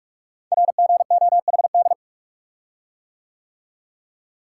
Morse code, heard as a keyed tone, about 44 wpm, 700 Hz. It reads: RGOHD
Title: Code Groups - 5 Letters - 45wpm